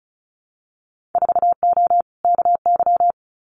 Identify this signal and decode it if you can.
Morse code, heard as a keyed tone, about 34 wpm, 700 Hz. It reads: 4O KY
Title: Winter Field Day Exchanges Encore 35wpm